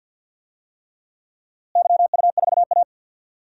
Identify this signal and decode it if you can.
Morse code, heard as a keyed tone, about 50 wpm, 700 Hz. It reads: YU4A
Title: DXCC-100 Call Signs - 4 Characters - Repeated 3x - 50wpm